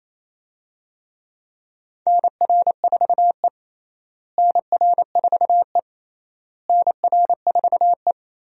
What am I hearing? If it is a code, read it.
Morse code, heard as a keyed tone, about 28 wpm, 700 Hz. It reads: NR4E NR4E NR4E
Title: USA Call Signs - 4 Characters - Repeated 3x - 28wpm